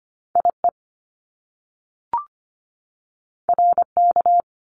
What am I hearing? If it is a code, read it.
Morse code, heard as a keyed tone, about 25 wpm, 700 Hz. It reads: IE RK
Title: Top Two Letter Combinations in Words 25wpm